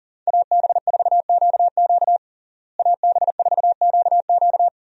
Morse code, heard as a keyed tone, about 40 wpm, 700 Hz. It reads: AB4QQ AB4QQ
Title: USA Call Signs - 5 Characters - Repeated 2x - 40wpm